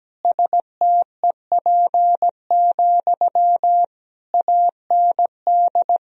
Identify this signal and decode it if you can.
Morse code, heard as a keyed tone, about 17 wpm, 700 Hz. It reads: STEP, AND
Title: Sentences from Top 1000 Words - Review Part 2 - 17wpm